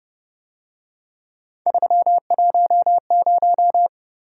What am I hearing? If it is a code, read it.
Morse code, heard as a keyed tone, about 29 wpm, 700 Hz. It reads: 310
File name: Speed-Racing - Three Letter-Number Combinations - 25wpm